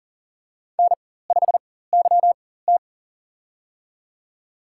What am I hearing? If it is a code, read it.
Morse code, heard as a keyed tone, about 38 wpm, spaced as for 18 wpm, 700 Hz. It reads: N5YT
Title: USA Call Signs - 4x Character Spacing - 40wpm